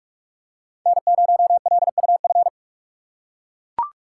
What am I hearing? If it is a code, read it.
Morse code, heard as a keyed tone, about 41 wpm, 700 Hz. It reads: N0LUF